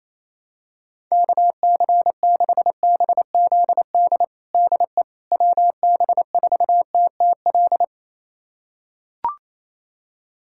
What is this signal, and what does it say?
Morse code, heard as a keyed tone, about 28 wpm, 700 Hz. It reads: KC6BZD DE WB4TTL
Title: USA Callsign DE Callsign - 6 Characters - 28wpm